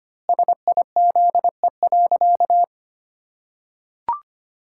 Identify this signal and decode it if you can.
Morse code, heard as a keyed tone, about 25 wpm, 700 Hz. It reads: SIZE.
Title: Pangram Challenge v2 - 25wpm